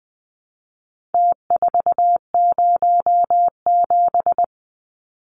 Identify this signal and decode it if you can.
Morse code, heard as a keyed tone, about 20 wpm, 700 Hz. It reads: T407
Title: Numbers - Contest Style - 4-Digit 20wpm